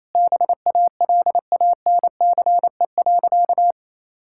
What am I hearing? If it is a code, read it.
Morse code, heard as a keyed tone, about 28 wpm, 700 Hz. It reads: BALANCE.
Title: Sentences from Top 1000 Words - Review Part 6 - 28wpm